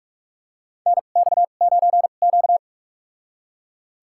Morse code, heard as a keyed tone, about 42 wpm, spaced as for 31 wpm, 700 Hz. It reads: NX9Q
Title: USA Call Signs - 4 Characters - 2x Character Spacing - 45wpm